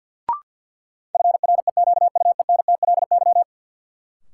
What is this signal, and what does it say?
Morse code, heard as a keyed tone, about 51 wpm, 700 Hz. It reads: FREQUENTLY
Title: Sentences from Top 1300 Words - New Word Drill Alt - 50wpm